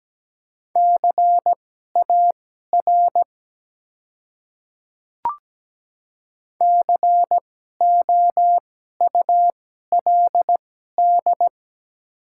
Morse code, heard as a keyed tone, about 17 wpm, spaced as for 12 wpm, 700 Hz. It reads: CAR COULD
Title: 2x Character Spacing - Top 300 Words - 17wpm